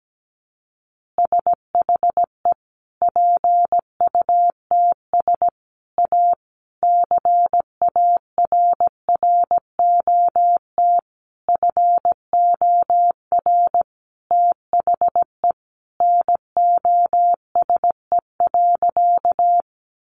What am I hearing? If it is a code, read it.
Morse code, heard as a keyed tone, about 17 wpm, 700 Hz. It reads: SHE PUTS A CARROT FOR THE NOSE.
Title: Easy Phrases 17wpm